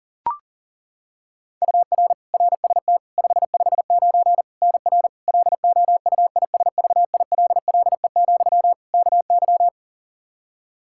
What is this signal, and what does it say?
Morse code, heard as a keyed tone, about 40 wpm, 700 Hz. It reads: UR RST 559 NR LOUISVILLE, KY
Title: K1USN SST QSO Practice 40wpm